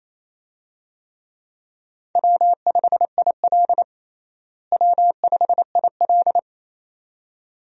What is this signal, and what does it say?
Morse code, heard as a keyed tone, about 28 wpm, 700 Hz. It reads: W5IL W5IL
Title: USA Call Signs - 4 Characters - Repeated 2x - 28wpm